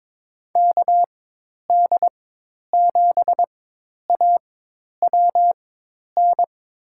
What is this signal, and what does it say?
Morse code, heard as a keyed tone, about 22 wpm, spaced as for 10 wpm, 700 Hz. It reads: KD7AWN
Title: USA Call Signs - 6 Characters - 4x Character Spacing - 22wpm